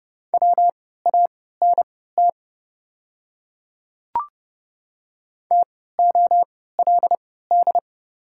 Morse code, heard as a keyed tone, about 31 wpm, spaced as for 17 wpm, 700 Hz. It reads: WANT TOLD
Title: 3x Character Spacing - Top 500 Words - 30wpm